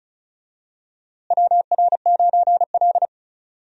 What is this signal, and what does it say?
Morse code, heard as a keyed tone, about 36 wpm, 700 Hz. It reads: WR9L
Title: USA Call Signs - 4 Characters - Repeated 3x - 35wpm